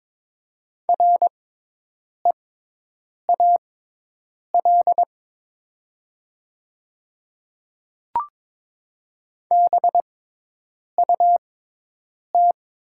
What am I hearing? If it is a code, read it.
Morse code, heard as a keyed tone, about 22 wpm, spaced as for 8 wpm, 700 Hz. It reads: REAL BUT
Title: 6x Character Spacing - Top 300 Words - 22wpm